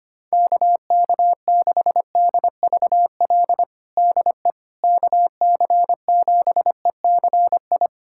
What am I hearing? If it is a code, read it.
Morse code, heard as a keyed tone, about 25 wpm, 700 Hz. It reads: KK6DVL DE KC7ECI
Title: USA Callsign DE Callsign - 6 Characters - 25wpm